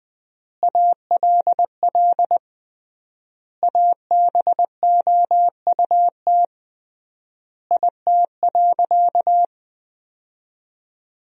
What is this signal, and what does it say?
Morse code, heard as a keyed tone, about 20 wpm, 700 Hz. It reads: ALL ABOUT IT.
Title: 3x Word-Spacing - Sentences from Top 200 Words - 20wpm